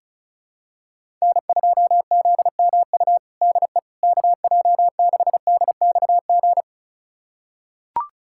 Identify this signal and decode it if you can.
Morse code, heard as a keyed tone, about 36 wpm, 700 Hz. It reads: N2ZMU DE KJ6DXG
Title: USA Callsign DE Callsign 35wpm